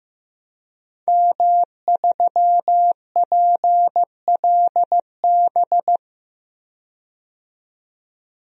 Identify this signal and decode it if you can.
Morse code, heard as a keyed tone, about 15 wpm, 700 Hz. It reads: M3PLB
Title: DXCC-100 Call Signs - 5 Characters - 15wpm